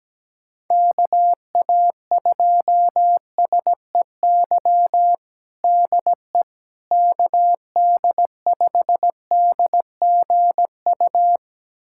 Morse code, heard as a keyed tone, about 17 wpm, 700 Hz. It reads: KA2SEY DE KD5DGU
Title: USA Callsign DE Callsign 17wpm